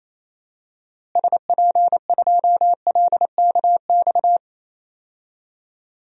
Morse code, heard as a keyed tone, about 28 wpm, 700 Hz. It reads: SP2LKX
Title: DXCC-100 Call Signs - 6 Characters - Repeated 3x - 28wpm